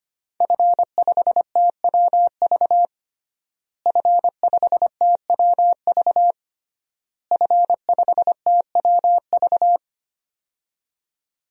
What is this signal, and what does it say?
Morse code, heard as a keyed tone, about 25 wpm, 700 Hz. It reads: F5TWV F5TWV F5TWV
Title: DXCC-100 Call Signs - 5 Characters - Repeated 3x - 25wpm